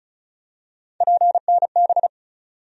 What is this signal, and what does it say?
Morse code, heard as a keyed tone, about 36 wpm, 700 Hz. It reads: PNB
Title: USA Call Sign Postfixes - 3 Letters 35wpm